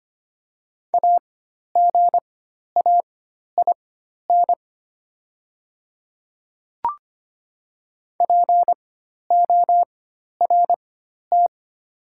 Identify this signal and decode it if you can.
Morse code, heard as a keyed tone, about 25 wpm, spaced as for 12 wpm, 700 Hz. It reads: AGAIN PORT
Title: 4x Character Spacing - Top 300 Words - 25wpm